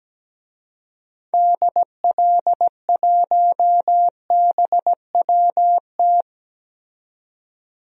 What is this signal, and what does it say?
Morse code, heard as a keyed tone, about 17 wpm, 700 Hz. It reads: DL1BWT
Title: DXCC-100 Call Signs - 6 Characters - 17wpm